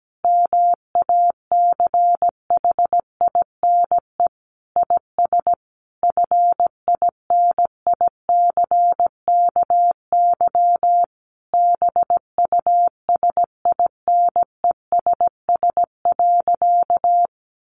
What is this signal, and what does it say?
Morse code, heard as a keyed tone, about 17 wpm, 700 Hz. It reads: MACHINE IS FINICKY BUSINESS.
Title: A Sound Of Thunder 17wpm